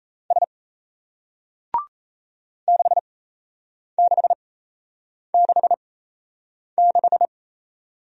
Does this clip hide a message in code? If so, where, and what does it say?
Yes: Morse code from 0.3 to 7.2 s, heard as a keyed tone, about 37 wpm, spaced as for 8 wpm, 700 Hz. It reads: S 6666